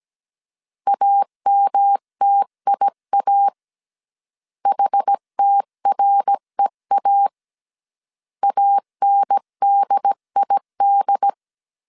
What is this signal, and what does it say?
Morse code, heard as a keyed tone, about 17 wpm, 800 Hz. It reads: AMTIA HTREA ANDID
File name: CW / DET / 1 3 1500C S A 16wpm 2141